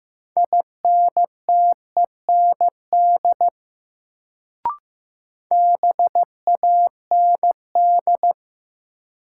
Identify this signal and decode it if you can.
Morse code, heard as a keyed tone, about 15 wpm, 700 Hz. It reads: INTEND BAND
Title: Sentences from Top 1600 Words - New Word Drill - 15wpm